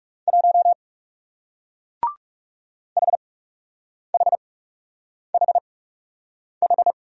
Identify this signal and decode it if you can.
Morse code, heard as a keyed tone, about 40 wpm, spaced as for 8 wpm, 700 Hz. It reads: 1 HHHH